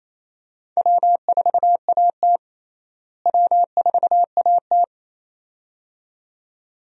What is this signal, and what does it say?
Morse code, heard as a keyed tone, about 28 wpm, 700 Hz. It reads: W4AT W4AT
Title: USA Call Signs - 4 Characters - Repeated 2x - 28wpm